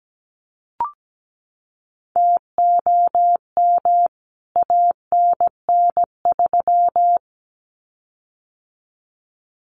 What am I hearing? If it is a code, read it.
Morse code, heard as a keyed tone, about 17 wpm, 700 Hz. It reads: TOM ANN3
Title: CWT Exchange 17wpm